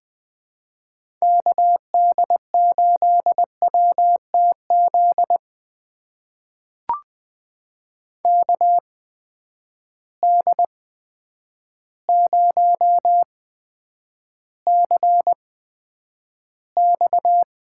Morse code, heard as a keyed tone, about 20 wpm, spaced as for 9 wpm, 700 Hz. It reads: KD8WTZ KD0CX
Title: USA Call Signs - 6 Characters - 8x Character Spacing - 20wpm